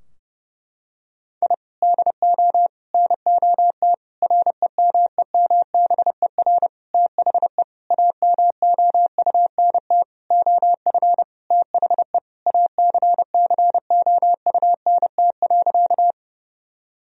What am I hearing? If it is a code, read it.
Morse code, heard as a keyed tone, about 30 wpm, 700 Hz. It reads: I DO NOT REMEMBER THE AMOUNT OF THE ACCOUNT.
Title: Sentences from Top 1000 Words - Review Part 4 - 30wpm